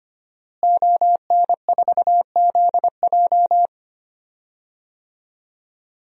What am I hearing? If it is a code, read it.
Morse code, heard as a keyed tone, about 25 wpm, 700 Hz. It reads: ON4ZJ
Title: DXCC-100 Call Signs - 5 Characters - Repeated 2x - 25wpm